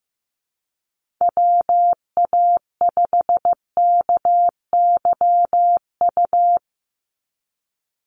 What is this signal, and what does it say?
Morse code, heard as a keyed tone, about 15 wpm, 700 Hz. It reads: WA5KYU
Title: USA Call Signs 15wpm